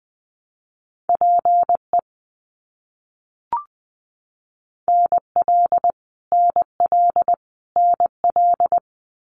Morse code, heard as a keyed tone, about 20 wpm, 700 Hz. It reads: PE NL NL NL
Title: Canadian Provinces and Territories - Repeated 3x - 20wpm